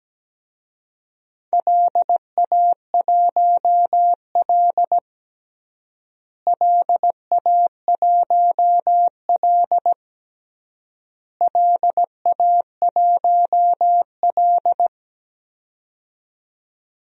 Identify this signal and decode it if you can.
Morse code, heard as a keyed tone, about 17 wpm, 700 Hz. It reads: LA1L LA1L LA1L
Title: DXCC-100 Call Signs - 4 Characters - Repeated 3x - 17wpm